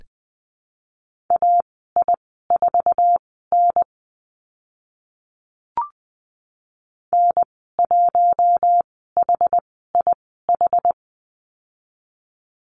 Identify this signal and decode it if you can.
Morse code, heard as a keyed tone, about 20 wpm, spaced as for 14 wpm, 700 Hz. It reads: AI4N N1HIH
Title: 062 Lesson 11g - Letter h - Call Signs 20wpm